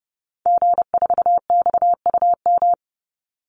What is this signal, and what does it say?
Morse code, heard as a keyed tone, about 30 wpm, 700 Hz. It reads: G4XUM
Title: Call Signs - 30wpm